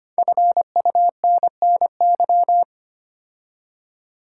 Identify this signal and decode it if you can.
Morse code, heard as a keyed tone, about 25 wpm, 700 Hz. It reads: FUNNY
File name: 5 Letter Words - First 3 Letters Given - 25wpm